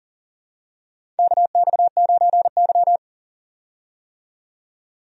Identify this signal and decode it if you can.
Morse code, heard as a keyed tone, about 40 wpm, 700 Hz. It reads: KX9Y
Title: USA Call Signs - 4 Characters - 40wpm